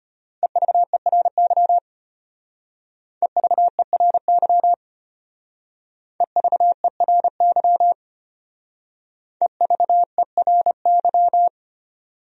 Morse code, heard as a keyed tone, about 32 wpm, 700 Hz. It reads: EVERY EVERY EVERY EVERY